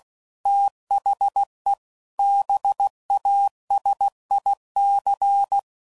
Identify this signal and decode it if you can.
Morse code, heard as a keyed tone, about 16 wpm, 800 Hz. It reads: THE BASIC
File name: CW Code Groups @15WPM / Technical Subjects / AND Gate